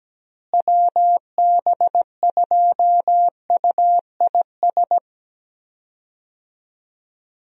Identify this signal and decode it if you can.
Morse code, heard as a keyed tone, about 17 wpm, 700 Hz. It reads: WB2UIS
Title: USA Call Signs - 6 Characters - Repeated 2x - 17wpm